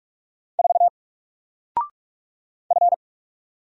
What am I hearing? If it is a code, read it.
Morse code, heard as a keyed tone, about 44 wpm, 700 Hz. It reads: 4 F
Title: Single Character (Letter:Number:Punctuation:Prosign) 45wpm